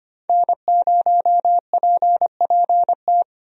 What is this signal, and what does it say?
Morse code, heard as a keyed tone, about 25 wpm, 700 Hz. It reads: N0PPT
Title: USA Call Signs - 5 Characters - Repeated 3x - 25wpm